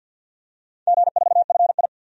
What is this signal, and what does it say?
Morse code, heard as a keyed tone, about 51 wpm, 700 Hz. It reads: G4FI
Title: DXCC-100 Call Signs - 4 Characters - 50wpm